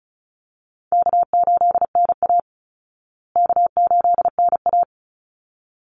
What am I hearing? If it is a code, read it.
Morse code, heard as a keyed tone, about 34 wpm, 700 Hz. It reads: K8NA K8NA
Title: Easy Call Signs Repeated 2x - 35wpm